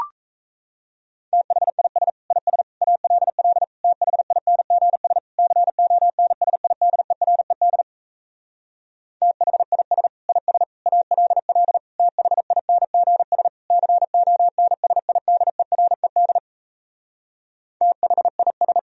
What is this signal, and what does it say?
Morse code, heard as a keyed tone, about 38 wpm, 700 Hz. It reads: THIS IS ALL THINGS CONSIDERED THIS IS ALL THINGS CONSIDERED THIS